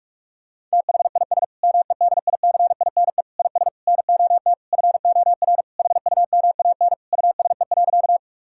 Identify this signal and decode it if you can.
Morse code, heard as a keyed tone, about 46 wpm, 700 Hz. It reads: THIS MEDICINE IS NOT FOR HUMAN USE.
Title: Sentences from Top 1000 Words - Review Part 6 - 45wpm